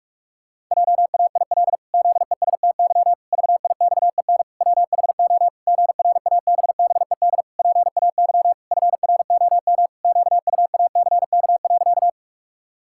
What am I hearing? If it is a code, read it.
Morse code, heard as a keyed tone, about 41 wpm, 700 Hz. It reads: JAIL ZESTY VIXEN WHO GRABBED PAY FROM QUACK.
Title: Pangram Challenge v2 - 45wpm